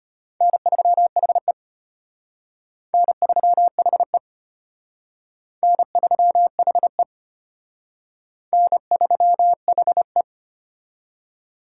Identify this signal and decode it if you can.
Morse code, heard as a keyed tone, about 32 wpm, 700 Hz. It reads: N3HE N3HE N3HE N3HE